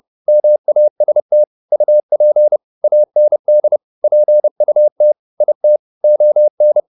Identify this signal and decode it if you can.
Morse code, heard as a keyed tone, about 30 wpm, 600 Hz. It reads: MAST UP AND PUT IT ON